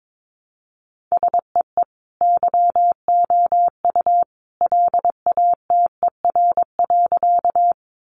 Morse code, heard as a keyed tone, about 22 wpm, 700 Hz. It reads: SEE YOU LATER.
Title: Sentences from Top 700 Words 22wpm